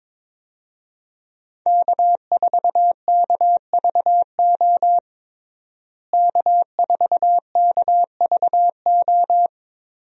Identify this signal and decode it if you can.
Morse code, heard as a keyed tone, about 22 wpm, 700 Hz. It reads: K4KVO K4KVO
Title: USA Call Signs - 5 Characters - Repeated 3x - 22wpm